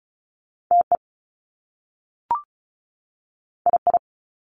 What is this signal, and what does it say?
Morse code, heard as a keyed tone, about 36 wpm, 700 Hz. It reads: TE II
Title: Top Two Letter Combinations in Words 35wpm